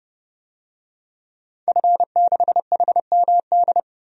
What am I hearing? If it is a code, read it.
Morse code, heard as a keyed tone, about 30 wpm, 700 Hz. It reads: F6HMD
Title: DXCC-100 Call Signs - 5 Characters - 30wpm